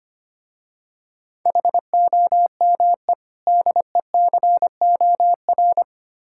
Morse code, heard as a keyed tone, about 26 wpm, 700 Hz. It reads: HOME DECOR
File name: practice / 25wpm / Difficult Words - Speed-Racing with Hints - 25wpm